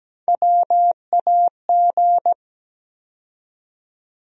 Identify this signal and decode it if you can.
Morse code, heard as a keyed tone, about 17 wpm, 700 Hz. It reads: WAG
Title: USA Call Sign Postfixes - 3 Letters 17wpm